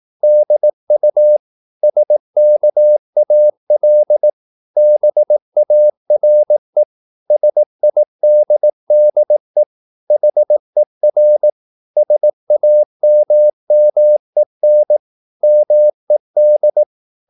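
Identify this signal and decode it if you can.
Morse code, heard as a keyed tone, about 18 wpm, 600 Hz. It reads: DU SKAL BARE SIDDE HER SAMMEN MED
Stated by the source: Slow (QRS) Morse practice at 18 wpm, CW med Gnister.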